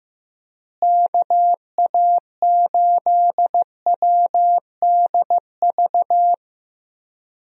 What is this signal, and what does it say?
Morse code, heard as a keyed tone, about 15 wpm, 700 Hz. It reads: KA8WDV
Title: USA Call Signs - 6 Characters - Repeated 3x - 15wpm